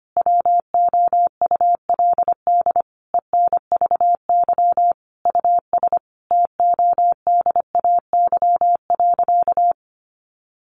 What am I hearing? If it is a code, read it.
Morse code, heard as a keyed tone, about 25 wpm, 700 Hz. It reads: WOULD ENVY US TODAY.
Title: A Sound Of Thunder 25wpm